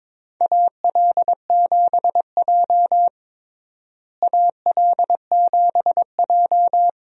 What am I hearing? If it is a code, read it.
Morse code, heard as a keyed tone, about 21 wpm, 700 Hz. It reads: AL7J AL7J
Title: USA Call Signs - 4 Characters - Repeated 2x - 22wpm